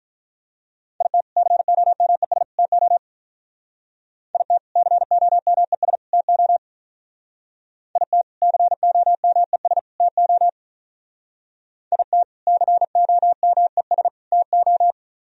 Morse code, heard as a keyed tone, about 42 wpm, 700 Hz. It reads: IT COMES TO IT COMES TO IT COMES TO IT COMES TO
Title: Speed-Racing - Sets of 3 Words 35wpm